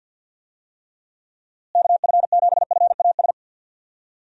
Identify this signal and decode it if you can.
Morse code, heard as a keyed tone, about 45 wpm, 700 Hz. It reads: KF7FAS
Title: USA Call Signs - 6 Characters - Repeated 2x - 50wpm